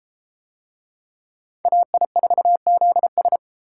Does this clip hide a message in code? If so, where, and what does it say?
Yes: Morse code from 1.7 to 3.4 s, heard as a keyed tone, about 32 wpm, 700 Hz. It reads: AI4ZS